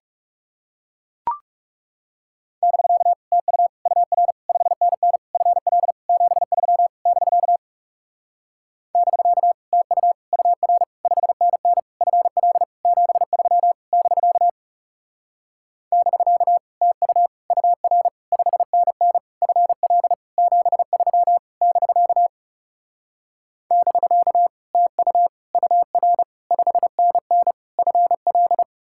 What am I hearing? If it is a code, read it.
Morse code, heard as a keyed tone, about 37 wpm, 700 Hz. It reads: <BK> TU UR 5NN FL 73 <BK> <BK> TU UR 5NN FL 73 <BK> <BK> TU UR 5NN FL 73 <BK> <BK> TU UR 5NN FL
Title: Speed-Racing Common QSO Phrases - 30wpm